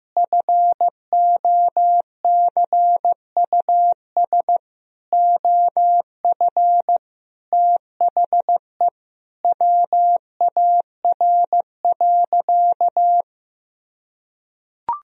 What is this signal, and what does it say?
Morse code, heard as a keyed tone, about 15 wpm, 700 Hz. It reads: FOCUS OF THE WAR.
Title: Sentences from Top 1000 Words - Review Part 4 - 15wpm